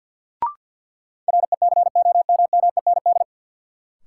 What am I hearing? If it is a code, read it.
Morse code, heard as a keyed tone, about 52 wpm, 700 Hz. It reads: RECOMMEND
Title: Sentences from Top 1200 Words - New Word Drill - 50wpm